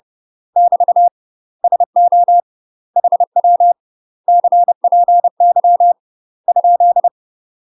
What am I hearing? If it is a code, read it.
Morse code, heard as a keyed tone, about 30 wpm, 700 Hz. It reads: <BT> SO HW CPY ?
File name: UK MASTER QSOs TRAINING / 30WPM / UK MASTER QSOs 0000